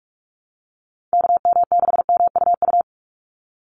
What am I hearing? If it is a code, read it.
Morse code, heard as a keyed tone, about 44 wpm, 700 Hz. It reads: KM6MUU
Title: USA Call Signs 45wpm